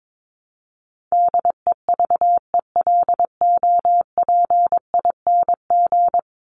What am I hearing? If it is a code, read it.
Morse code, heard as a keyed tone, about 22 wpm, 700 Hz. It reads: DEVELOPING
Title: ING Suffix - Top 100 Words with Base Words 22wpm